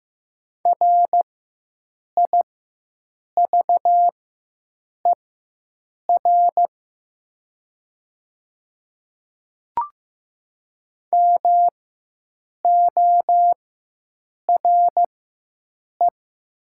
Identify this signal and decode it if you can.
Morse code, heard as a keyed tone, about 15 wpm, spaced as for 7 wpm, 700 Hz. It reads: RIVER MORE
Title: 4x Character Spacing - Top 300 Words - 15wpm